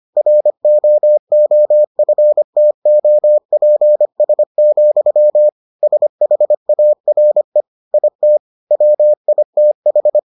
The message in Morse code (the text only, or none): ROOFTOPS, SHARE IT WITH